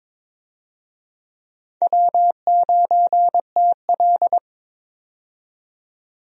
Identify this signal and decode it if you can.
Morse code, heard as a keyed tone, about 22 wpm, 700 Hz. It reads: W9TL
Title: USA Call Signs - 4 Characters - 22wpm